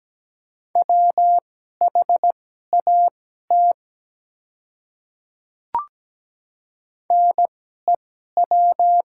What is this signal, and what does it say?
Morse code, heard as a keyed tone, about 17 wpm, spaced as for 12 wpm, 700 Hz. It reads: WHAT NEW
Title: 2x Character Spacing - Top 300 Words - 17wpm